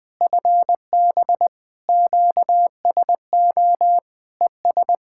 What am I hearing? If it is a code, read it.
Morse code, heard as a keyed tone, about 20 wpm, 700 Hz. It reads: FB QSO ES
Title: Practice QSO - Exchange 4 - Ending - 20wpm